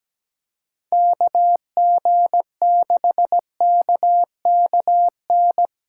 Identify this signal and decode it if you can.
Morse code, heard as a keyed tone, about 17 wpm, 700 Hz. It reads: KG6KKN
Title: USA Callsign DE Callsign - 6 Characters - 17wpm